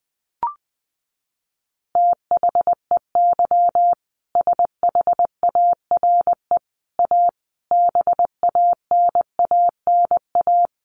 Morse code, heard as a keyed tone, about 19 wpm, 700 Hz. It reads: THEY SHARE A BANANA
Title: Additional Easy Phrases 20wpm